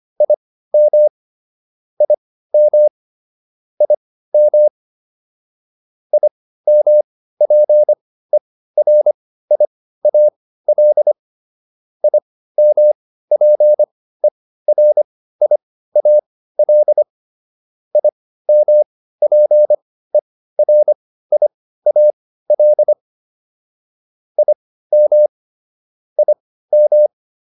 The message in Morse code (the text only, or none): IM IM IM IMPERIAL IMPERIAL IMPERIAL IM IM